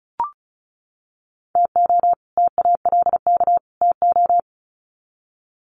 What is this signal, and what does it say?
Morse code, heard as a keyed tone, about 34 wpm, 700 Hz. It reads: TO TALK TO
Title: Sets of 3 Words 35wpm